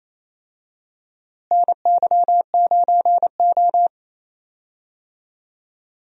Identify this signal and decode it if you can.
Morse code, heard as a keyed tone, about 28 wpm, 700 Hz. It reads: NY9O
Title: USA Call Signs - 4 Characters - Repeated 3x - 28wpm